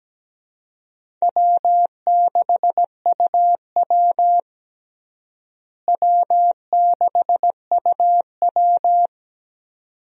Morse code, heard as a keyed tone, about 17 wpm, 700 Hz. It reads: W6UW W6UW
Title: USA Call Signs - 4 Characters - Repeated 2x - 17wpm